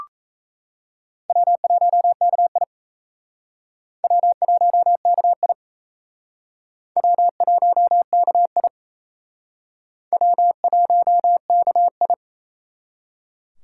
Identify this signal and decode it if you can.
Morse code, heard as a keyed tone, about 35 wpm, 700 Hz. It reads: W1KI W1KI W1KI W1KI
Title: Speed Racing - USA 1x2 Call Signs 28wpm